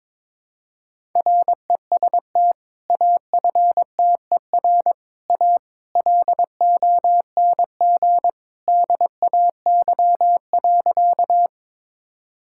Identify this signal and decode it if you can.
Morse code, heard as a keyed tone, about 22 wpm, 700 Hz. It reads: REST AFTER A LONG DAY.
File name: practice / 22wpm / Sentences with 2-Word Phrases - 22wpm